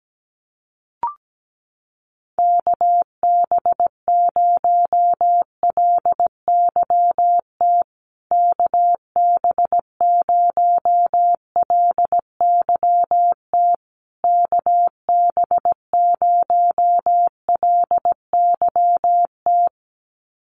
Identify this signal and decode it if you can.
Morse code, heard as a keyed tone, about 17 wpm, 700 Hz. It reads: KB0LYT KB0LYT KB0LYT
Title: USA Call Signs Repeated 3x 17wpm